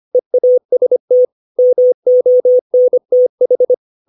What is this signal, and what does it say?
Morse code, heard as a keyed tone, about 25 wpm, 500 Hz. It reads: EAST MONTH
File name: 632 words / 632 Words 25WPM